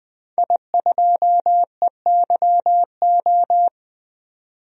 Morse code, heard as a keyed tone, about 20 wpm, 700 Hz. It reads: I2EYO
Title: DXCC-100 Call Signs - 5 Characters - 20wpm